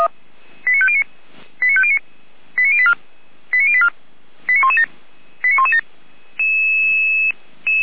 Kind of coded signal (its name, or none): five-tone selective call